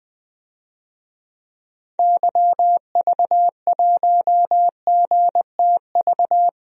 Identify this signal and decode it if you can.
Morse code, heard as a keyed tone, about 20 wpm, 700 Hz. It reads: YV1GTV
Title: DXCC-100 Call Signs - 6 Characters - Repeated 2x - 20wpm